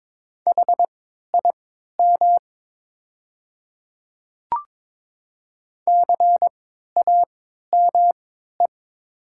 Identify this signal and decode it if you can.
Morse code, heard as a keyed tone, about 22 wpm, spaced as for 12 wpm, 690 Hz. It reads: HIM CAME
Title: 3x Character Spacing - Top 200 Words - 22wpm